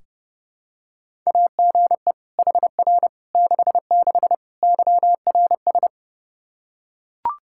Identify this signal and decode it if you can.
Morse code, heard as a keyed tone, about 32 wpm, 700 Hz. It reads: AGE HR 66 YRS
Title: Practice QSO - Exchange 3 - More Info - 30wpm